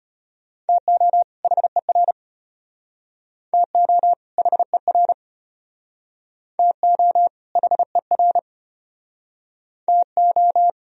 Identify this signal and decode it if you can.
Morse code, heard as a keyed tone, about 32 wpm, 700 Hz. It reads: TO HER TO HER TO HER TO